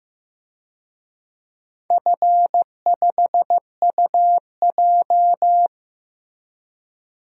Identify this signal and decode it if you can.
Morse code, heard as a keyed tone, about 15 wpm, 700 Hz. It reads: F5UJ
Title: DXCC-100 Call Signs - 4 Characters - Repeated 2x - 15wpm